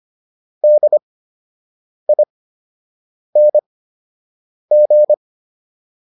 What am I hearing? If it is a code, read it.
Morse code, heard as a keyed tone, about 25 wpm, spaced as for 7 wpm, 600 Hz. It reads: DING